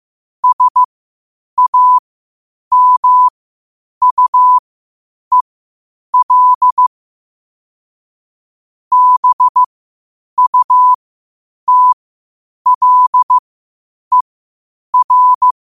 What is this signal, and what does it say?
Morse code, heard as a keyed tone, about 15 wpm, spaced as for 8 wpm, 1000 Hz. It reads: SAMUEL BUTLER
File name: qotdfiles / 8wpm